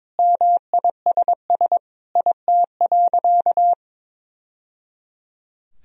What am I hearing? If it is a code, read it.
Morse code, heard as a keyed tone, about 21 wpm, 700 Hz. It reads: MISS IT.
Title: Sentences from Top 1000 Words - Review Part 3 - 22wpm